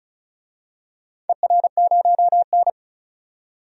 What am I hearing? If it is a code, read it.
Morse code, heard as a keyed tone, about 36 wpm, 690 Hz. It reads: ER0N
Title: DXCC-100 Call Signs - 4 Characters - 35wpm